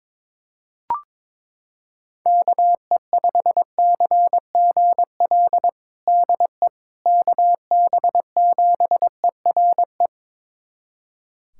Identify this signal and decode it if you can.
Morse code, heard as a keyed tone, about 21 wpm, 700 Hz. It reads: KE5CGL DE KB7ERE
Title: USA Callsign DE Callsign - 6 Characters - 22wpm